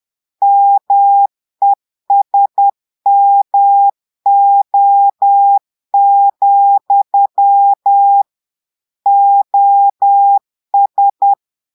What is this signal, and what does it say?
Morse code, heard as a keyed tone, about 10 wpm, 800 Hz. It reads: MESMO, OS
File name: portugais texte en clair 2 vitesse 10 mots minutes